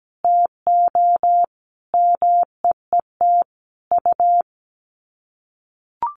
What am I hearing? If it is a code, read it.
Morse code, heard as a keyed tone, about 17 wpm, 700 Hz. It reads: TO MEET U
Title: Common QSO Phrases 17wpm